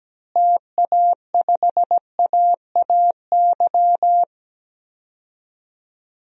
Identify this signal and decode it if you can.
Morse code, heard as a keyed tone, about 17 wpm, 700 Hz. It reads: TA5AAY
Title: DXCC-100 Call Signs - 6 Characters - Repeated 2x - 17wpm